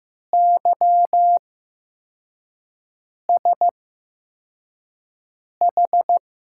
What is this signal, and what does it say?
Morse code, heard as a keyed tone, about 15 wpm, spaced as for 4 wpm, 700 Hz. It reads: YSH
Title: USA Call Signs - 6 Characters - 8x Character Spacing - 15wpm